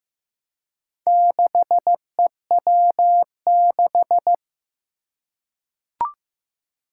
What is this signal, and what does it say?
Morse code, heard as a keyed tone, about 15 wpm, 700 Hz. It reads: 6EW6
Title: Four Challenging Letter-Number Combinations 15wpm